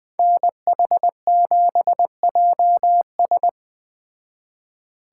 Morse code, heard as a keyed tone, about 21 wpm, 700 Hz. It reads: NH7JS